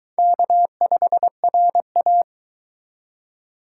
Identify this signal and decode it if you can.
Morse code, heard as a keyed tone, about 22 wpm, 700 Hz. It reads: K5RA